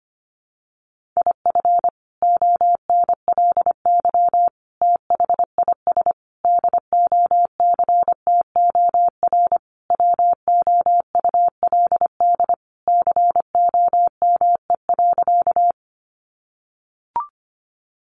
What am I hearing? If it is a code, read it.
Morse code, heard as a keyed tone, about 25 wpm, 700 Hz. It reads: IF ONLY THIS DOCTOR WOULD COME.
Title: Sentences from Top 1500 Words 25wpm